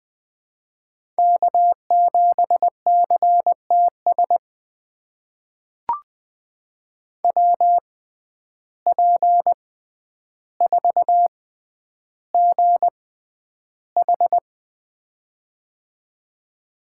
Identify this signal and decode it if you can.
Morse code, heard as a keyed tone, about 20 wpm, spaced as for 10 wpm, 700 Hz. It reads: K7CTS WP4GH
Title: USA Call Signs - 5 Characters - 6x Character Spacing - 20wpm